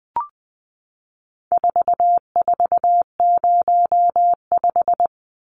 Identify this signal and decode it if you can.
Morse code, heard as a keyed tone, about 20 wpm, 700 Hz. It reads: 4405
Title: Numbers - 4-Digit 20wpm